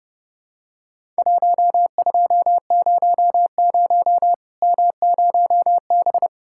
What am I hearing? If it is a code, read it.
Morse code, heard as a keyed tone, about 32 wpm, 700 Hz. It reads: 1200 M0B
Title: Worked All Europe DX Contest - QTC Exchange - 30wpm